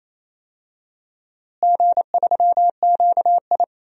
Morse code, heard as a keyed tone, about 28 wpm, 700 Hz. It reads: G3QI
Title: DXCC-100 Call Signs - 4 Characters - Repeated 3x - 28wpm